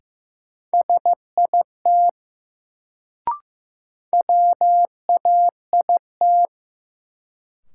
Morse code, heard as a keyed tone, about 15 wpm, 700 Hz. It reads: SIT WAIT
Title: Sentences from Top 500 Words - New Word Drill - 15wpm